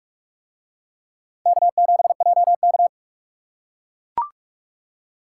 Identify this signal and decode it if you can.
Morse code, heard as a keyed tone, about 44 wpm, 700 Hz. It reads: K7JK